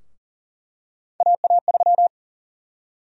Morse code, heard as a keyed tone, about 38 wpm, 700 Hz. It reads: AA3
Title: USA Call Sign Prefixes - 40wpm